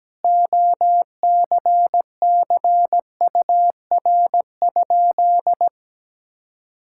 Morse code, heard as a keyed tone, about 17 wpm, 700 Hz. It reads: OCCUR?
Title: Sentences from Top 1000 Words - Review Part 6 - 17wpm